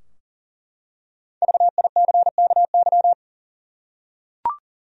Morse code, heard as a keyed tone, about 40 wpm, 700 Hz. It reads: VICKY
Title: USA Common Names - Max 5 Letters - 40wpm